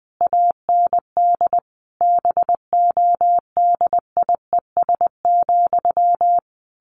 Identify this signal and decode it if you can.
Morse code, heard as a keyed tone, about 19 wpm, 700 Hz. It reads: AND BODIES,
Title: A Sound Of Thunder 20wpm